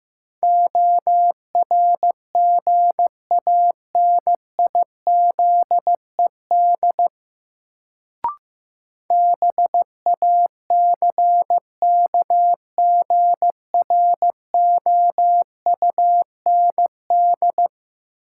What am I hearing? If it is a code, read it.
Morse code, heard as a keyed tone, about 15 wpm, 700 Hz. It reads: ORGANIZED BACKGROUND
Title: Sentences from Top 1300 Words - New Word Drill Alt - 15wpm